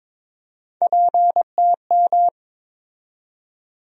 Morse code, heard as a keyed tone, about 22 wpm, 700 Hz. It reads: PTM
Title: USA Call Sign Postfixes - 3 Letters 22wpm